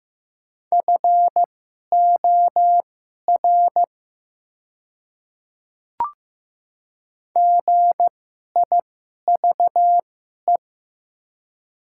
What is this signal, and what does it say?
Morse code, heard as a keyed tone, about 15 wpm, spaced as for 11 wpm, 700 Hz. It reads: FOR GIVE
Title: 2x Character Spacing - Top 300 Words - 15wpm